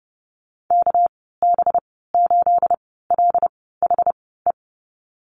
Morse code, heard as a keyed tone, about 30 wpm, spaced as for 17 wpm, 700 Hz. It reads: KB8LHE
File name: Lesson 37g - Letter x - Call Signs 30wpm